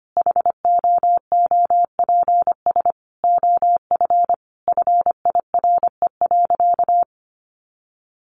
Morse code, heard as a keyed tone, about 25 wpm, 700 Hz. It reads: HOOPS OF FIRE.
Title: Easy Phrases 25wpm